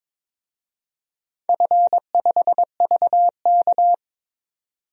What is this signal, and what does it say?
Morse code, heard as a keyed tone, about 22 wpm, 700 Hz. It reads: F5VK
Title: DXCC-100 Call Signs - 4 Characters - 22wpm